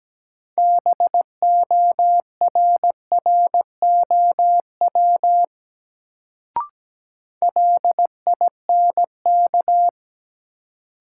Morse code, heard as a keyed tone, about 17 wpm, 700 Hz. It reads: BORROW LINK
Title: Sentences from Top 1500 Words - New Word Drill Alt - 17wpm